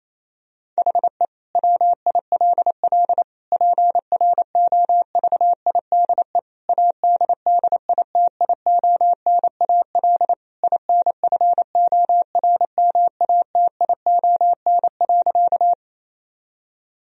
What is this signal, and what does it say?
Morse code, heard as a keyed tone, about 28 wpm, 700 Hz. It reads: HE WILL PROVIDE ADDITIONAL INFORMATION.
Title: Sentences from Top 1000 Words - Review Part 4 - 28wpm